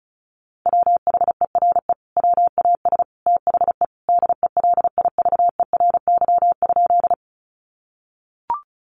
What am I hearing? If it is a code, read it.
Morse code, heard as a keyed tone, about 34 wpm, 700 Hz. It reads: WHERE WAS THE DELIVERY?
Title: Sentences from Top 1500 Words 35wpm